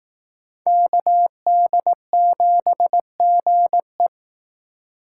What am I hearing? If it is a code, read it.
Morse code, heard as a keyed tone, about 19 wpm, 700 Hz. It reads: KD7GE